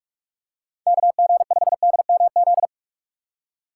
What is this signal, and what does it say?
Morse code, heard as a keyed tone, about 47 wpm, 700 Hz. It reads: KG5DMZ
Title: USA Call Signs - 6 Characters - 4x Character Spacing - 45wpm